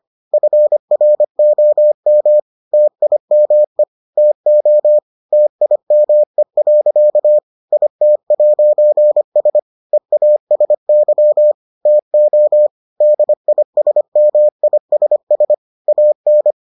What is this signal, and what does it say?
Morse code, heard as a keyed tone, about 25 wpm, 600 Hz. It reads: FROM TIME TO TIME. IT'S EASY TO DISMISS AN